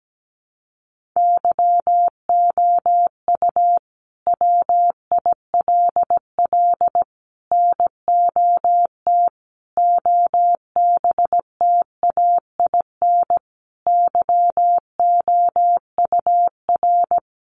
Morse code, heard as a keyed tone, about 17 wpm, 700 Hz. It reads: YOU WILL NOT OBTAIN YOUR
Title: Sentences from Top 1300 Words 17wpm